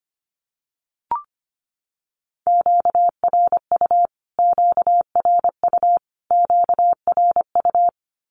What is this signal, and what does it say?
Morse code, heard as a keyed tone, about 25 wpm, 700 Hz. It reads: QRU QRU QRU
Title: Q-Codes Repeated 3x 25wpm